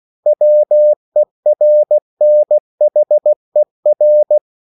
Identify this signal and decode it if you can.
Morse code, heard as a keyed tone, about 16 wpm, 600 Hz. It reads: WERNHER
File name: CW / Practice / EnglishQuotations0416